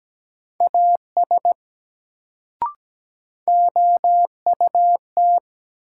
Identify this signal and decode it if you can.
Morse code, heard as a keyed tone, about 17 wpm, 700 Hz. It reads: AS OUT
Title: Sets of 2 Words - New Word Drill - 17wpm